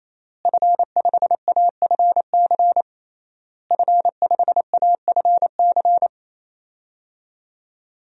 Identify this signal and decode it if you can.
Morse code, heard as a keyed tone, about 28 wpm, 700 Hz. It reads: F5AFC F5AFC
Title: DXCC-100 Call Signs - 5 Characters - Repeated 3x - 28wpm